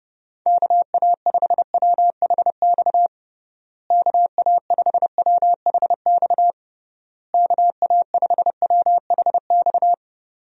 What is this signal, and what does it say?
Morse code, heard as a keyed tone, about 30 wpm, 700 Hz. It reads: KA5WHX KA5WHX KA5WHX
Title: USA Call Signs - 6 Characters - Repeated 3x - 30wpm